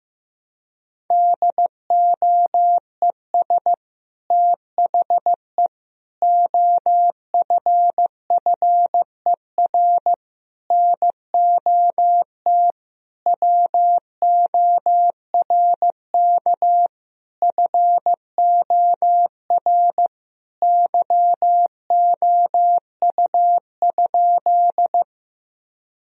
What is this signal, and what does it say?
Morse code, heard as a keyed tone, about 16 wpm, 700 Hz. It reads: DOES THE OFFER NOT WORK FOR YOU?
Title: Sentences from Top 1000 Words - Review Part 4 - 15wpm